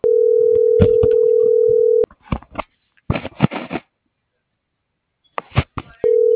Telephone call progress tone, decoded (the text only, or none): ringback tone